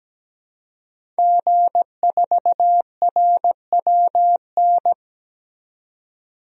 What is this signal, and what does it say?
Morse code, heard as a keyed tone, about 17 wpm, 700 Hz. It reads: G4RWN
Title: DXCC-100 Call Signs - 5 Characters - 17wpm